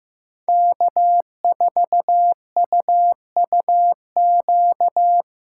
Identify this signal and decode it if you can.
Morse code, heard as a keyed tone, about 15 wpm, 700 Hz. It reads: K4UUQ
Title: USA Callsign DE Callsign - 5 Characters - 15wpm